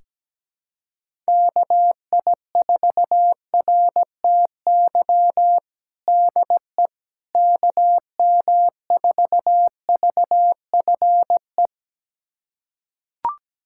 Morse code, heard as a keyed tone, about 17 wpm, 700 Hz. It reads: KI4RTY DE KM4VFE
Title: USA Callsign DE Callsign - 6 Characters - 17wpm